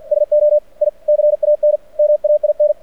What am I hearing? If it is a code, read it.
Morse code, heard as a keyed tone, about 49 wpm, 600 Hz. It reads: SO I CAN MAIN
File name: archived / CRSnet